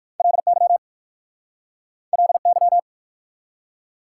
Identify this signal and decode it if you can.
Morse code, heard as a keyed tone, about 60 wpm, 700 Hz. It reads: LY LY